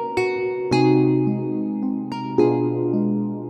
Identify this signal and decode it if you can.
There is no code.